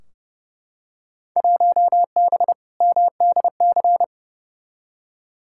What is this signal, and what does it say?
Morse code, heard as a keyed tone, about 30 wpm, 700 Hz. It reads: 1B MDC
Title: Field Day Exchanges - 30wpm